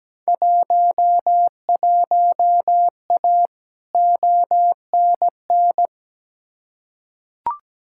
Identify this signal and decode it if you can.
Morse code, heard as a keyed tone, about 17 wpm, 700 Hz. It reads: 11A ONN
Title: Field Day Exchanges Encore - 17wpm